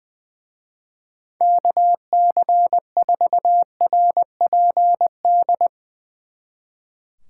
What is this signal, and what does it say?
Morse code, heard as a keyed tone, about 22 wpm, 700 Hz. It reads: KC4RPD
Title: Speed Racing - USA 2x3 Call Signs 20wpm